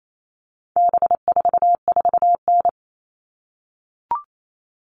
Morse code, heard as a keyed tone, about 28 wpm, 700 Hz. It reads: B44N
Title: All Cut Numbers - 4-Digit 28wpm